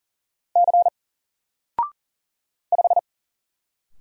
Morse code, heard as a keyed tone, about 40 wpm, 700 Hz. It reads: C 5
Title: Single Letter-Number 40wpm